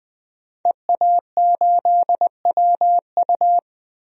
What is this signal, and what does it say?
Morse code, heard as a keyed tone, about 21 wpm, 700 Hz. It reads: EA8WU
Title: DXCC-100 Call Signs - 5 Characters - 20wpm